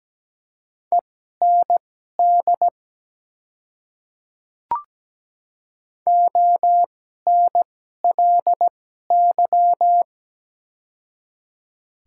Morse code, heard as a keyed tone, about 17 wpm, spaced as for 12 wpm, 700 Hz. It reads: END ONLY
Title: 2x Character Spacing - Top 300 Words - 17wpm